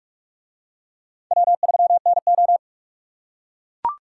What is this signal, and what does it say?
Morse code, heard as a keyed tone, about 44 wpm, 700 Hz. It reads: W3NO